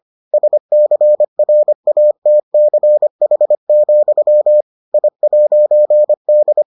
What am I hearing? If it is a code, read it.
Morse code, heard as a keyed tone, about 25 wpm, 600 Hz. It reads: SCRATCH, I'D